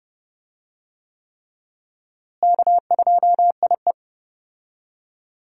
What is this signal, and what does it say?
Morse code, heard as a keyed tone, about 32 wpm, 700 Hz. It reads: K2IE
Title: USA Call Signs - 4 Characters - Repeated 3x - 30wpm